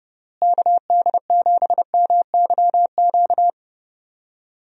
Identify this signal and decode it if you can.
Morse code, heard as a keyed tone, about 32 wpm, 700 Hz. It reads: KD7MYQ
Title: USA Call Signs - 6 Characters - Repeated 3x - 30wpm